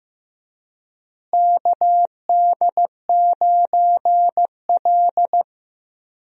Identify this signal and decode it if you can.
Morse code, heard as a keyed tone, about 15 wpm, 700 Hz. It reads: KD9L
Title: USA Call Signs - 4 Characters - Repeated 3x - 15wpm